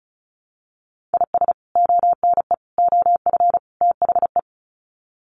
Morse code, heard as a keyed tone, about 34 wpm, 700 Hz. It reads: IS ONE OF THE
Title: Sets of 4 Words 35wpm